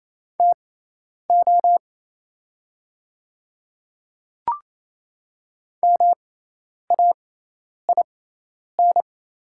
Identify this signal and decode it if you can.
Morse code, heard as a keyed tone, about 28 wpm, spaced as for 10 wpm, 700 Hz. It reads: TO MAIN
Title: 6x Character Spacing - Top 300 Words - 28wpm